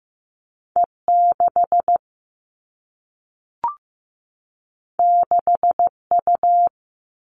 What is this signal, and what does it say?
Morse code, heard as a keyed tone, about 15 wpm, 700 Hz. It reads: E6 6U
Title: All Cut Numbers - 2-Digit 15wpm